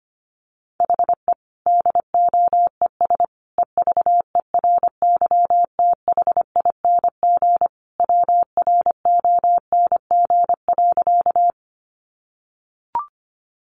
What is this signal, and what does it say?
Morse code, heard as a keyed tone, about 25 wpm, 700 Hz. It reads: HE DOES EVERYTHING WRONG.
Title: Sentences from Top 900 Words 25wpm